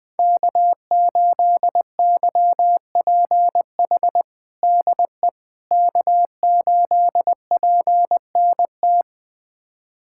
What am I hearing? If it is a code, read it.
Morse code, heard as a keyed tone, about 20 wpm, 700 Hz. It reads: K8YPH DE K8PNT
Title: USA Callsign DE Callsign - 5 Characters - 20wpm